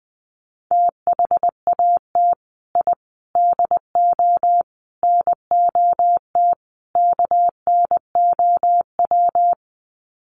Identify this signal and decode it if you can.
Morse code, heard as a keyed tone, about 20 wpm, 700 Hz. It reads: THAT I DO NOT KNOW
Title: Sets of 5 Words Encore2 20wpm